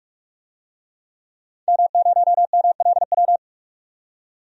Morse code, heard as a keyed tone, about 42 wpm, 700 Hz. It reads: M0MLW
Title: DXCC-100 Call Signs - 5 Characters - 45wpm